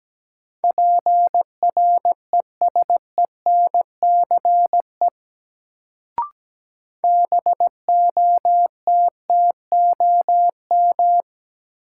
Morse code, heard as a keyed tone, about 17 wpm, 700 Hz. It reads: PRESENCE BOTTOM
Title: Sets of 3 Words - New Word Drill - 17wpm